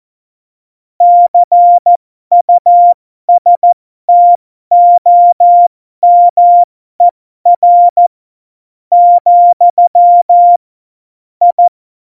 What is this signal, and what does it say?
Morse code, heard as a keyed tone, about 14 wpm, spaced as for 11 wpm, 700 Hz. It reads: CUSTOMER, I
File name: Morsekurs / Witze